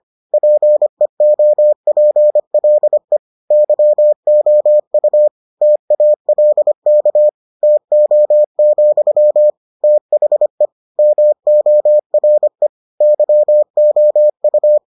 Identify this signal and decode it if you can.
Morse code, heard as a keyed tone, about 25 wpm, 600 Hz. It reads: PEOPLE YOU TALK TO, THE MORE YOU